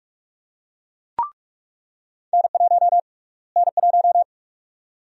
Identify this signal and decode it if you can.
Morse code, heard as a keyed tone, about 46 wpm, 700 Hz. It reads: N1 N1
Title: USA Call Sign Prefixes - Repeated 2x - 45wpm